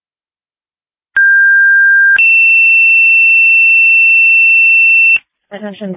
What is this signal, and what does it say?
Two-tone page, heard as a one-second tone, then a three-second tone, about 1600 then 2700 Hz, with speech. It is heard from 1.2 to 5.2 s.